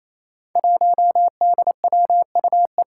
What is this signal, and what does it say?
Morse code, heard as a keyed tone, about 28 wpm, 700 Hz. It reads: 1DWUE
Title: Five Challenging Letter-Number Combinations 28wpm